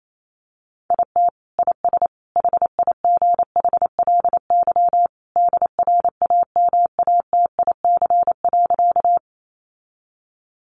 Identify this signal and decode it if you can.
Morse code, heard as a keyed tone, about 28 wpm, 700 Hz. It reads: IT IS HIGHLY DRAMATIC.
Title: Sentences from Top 1300 Words 28wpm